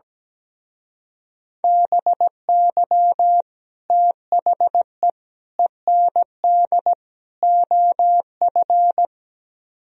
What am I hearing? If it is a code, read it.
Morse code, heard as a keyed tone, about 17 wpm, 700 Hz. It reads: BY THE END OF
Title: Speed-Racing - Sets of 4 Words 17wpm